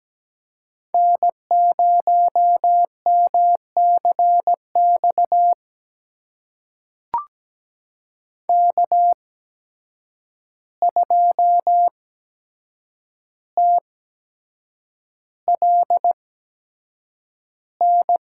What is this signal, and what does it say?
Morse code, heard as a keyed tone, about 17 wpm, spaced as for 7 wpm, 700 Hz. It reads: N0MCX K2TLN
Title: USA Call Signs - 5 Characters - 8x Character Spacing - 17wpm